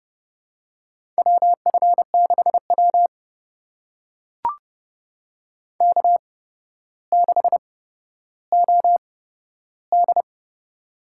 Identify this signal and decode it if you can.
Morse code, heard as a keyed tone, about 30 wpm, spaced as for 13 wpm, 700 Hz. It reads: WF6W K6OD
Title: USA Call Signs - 4 Characters - 8x Character Spacing - 30wpm